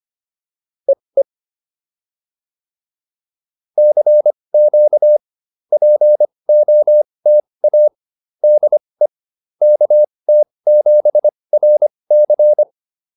Morse code, heard as a keyed tone, about 25 wpm, spaced as for 20 wpm, 590 Hz. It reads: EE CQ POTA DE KT7RC
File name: POTA207 20